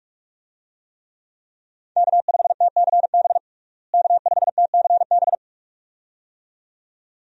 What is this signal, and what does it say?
Morse code, heard as a keyed tone, about 45 wpm, 700 Hz. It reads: K5TCB K5TCB
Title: USA Call Signs - 5 Characters - Repeated 2x - 45wpm